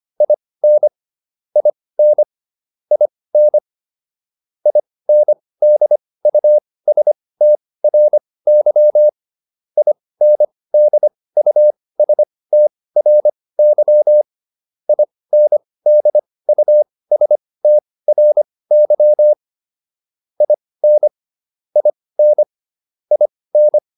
Morse code, heard as a keyed tone, about 25 wpm, spaced as for 18 wpm, 600 Hz. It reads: IN IN IN INDUSTRY INDUSTRY INDUSTRY IN IN IN